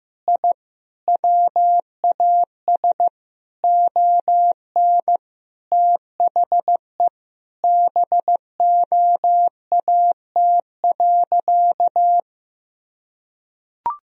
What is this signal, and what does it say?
Morse code, heard as a keyed tone, about 15 wpm, 700 Hz. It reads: I WAS ON THE BOAT.
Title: Sentences from Top 1000 Words - Review Part 3 - 15wpm